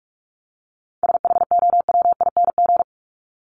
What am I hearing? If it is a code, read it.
Morse code, heard as a keyed tone, about 44 wpm, 700 Hz. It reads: SHOWING
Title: ING Suffix - Top 100 Words with Base Words 45wpm